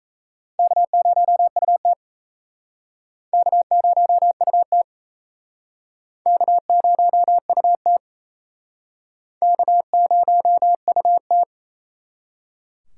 Morse code, heard as a keyed tone, about 38 wpm, 700 Hz. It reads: K0UT K0UT K0UT K0UT